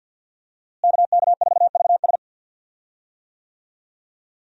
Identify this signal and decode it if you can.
Morse code, heard as a keyed tone, about 51 wpm, 700 Hz. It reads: KK4VS
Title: USA Call Signs - 5 Characters - Repeated 3x - 50wpm